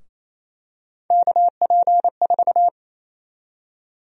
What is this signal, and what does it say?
Morse code, heard as a keyed tone, about 27 wpm, 700 Hz. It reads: KP4
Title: USA Call Sign Prefixes - Repeated 2x - 28wpm